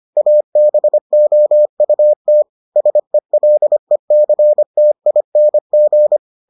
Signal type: Morse code

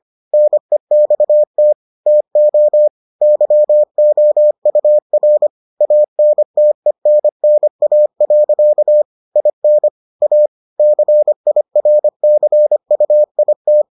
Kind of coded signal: Morse code

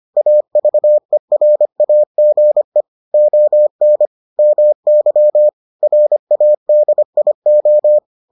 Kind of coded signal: Morse code

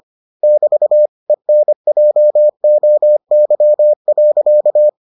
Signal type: Morse code

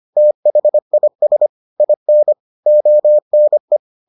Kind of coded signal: Morse code

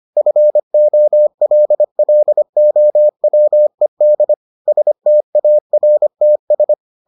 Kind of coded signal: Morse code

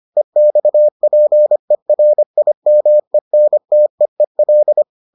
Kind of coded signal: Morse code

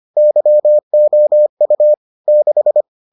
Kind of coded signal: Morse code

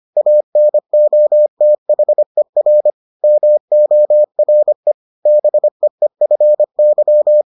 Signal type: Morse code

